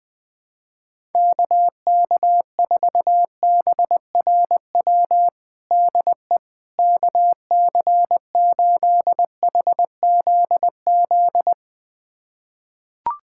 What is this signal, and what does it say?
Morse code, heard as a keyed tone, about 20 wpm, 700 Hz. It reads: KK4BRW DE KC8HZZ
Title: USA Callsign DE Callsign - 6 Characters - 20wpm